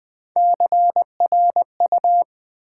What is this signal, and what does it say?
Morse code, heard as a keyed tone, about 20 wpm, 700 Hz. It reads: CRU
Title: Build A Word - 1 Letter At A Time - 5-Letter Words - 20wpm